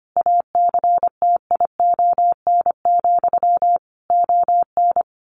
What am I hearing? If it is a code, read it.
Morse code, heard as a keyed tone, about 25 wpm, 700 Hz. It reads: ACTION, ON
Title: A Sound Of Thunder 25wpm